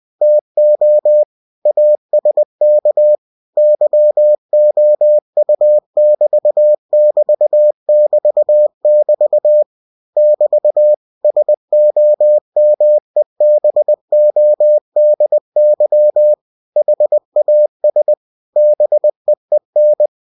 Morse code, heard as a keyed tone, about 20 wpm, 600 Hz. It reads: TO ASK YOU<BT><BT><BT><BT> <BT> SOMEBODY HAS BEEN
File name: cw-books / winnie-the-pooh / winnie-20wpm-page-0051